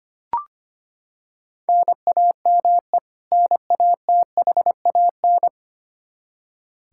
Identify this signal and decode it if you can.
Morse code, heard as a keyed tone, about 25 wpm, 700 Hz. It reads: NAME NATHAN
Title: Practice QSO - Exchange 1 - Beginning - 25wpm